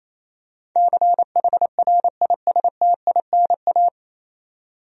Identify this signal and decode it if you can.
Morse code, heard as a keyed tone, about 28 wpm, 700 Hz. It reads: CHRISTINA
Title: USA Common Names - 28wpm